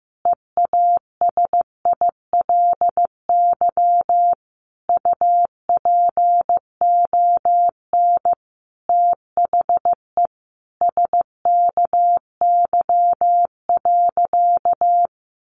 Morse code, heard as a keyed tone, about 15 wpm, 700 Hz. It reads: EASILY UPON THE SKY.
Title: A Sound Of Thunder 15wpm